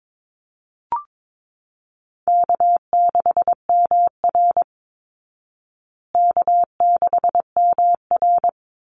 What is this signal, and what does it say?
Morse code, heard as a keyed tone, about 22 wpm, 700 Hz. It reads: K6MR K6MR
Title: Easy Call Signs Repeated 3x - 22wpm